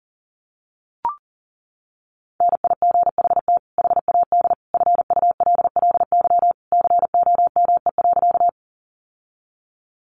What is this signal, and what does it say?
Morse code, heard as a keyed tone, about 40 wpm, 700 Hz. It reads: NIGHT HAD FULLY COME.
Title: Sentences from Top 1300 Words 40wpm